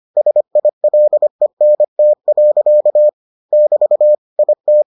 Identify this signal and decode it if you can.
Morse code, heard as a keyed tone, about 25 wpm, 600 Hz. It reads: SILENT. <BT> IT